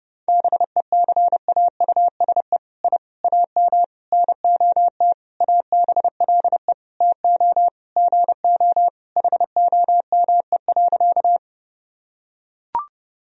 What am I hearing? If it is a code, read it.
Morse code, heard as a keyed tone, about 32 wpm, 700 Hz. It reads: BECAUSE I AM NOT ABLE TO GO HOME.
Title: Sentences from Top 1000 Words - Review Part 3 - 30wpm